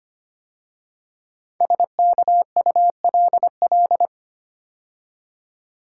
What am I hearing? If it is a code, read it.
Morse code, heard as a keyed tone, about 25 wpm, 700 Hz. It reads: SKULL
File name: 5 Letter Words - First 3 Letters Given - 25wpm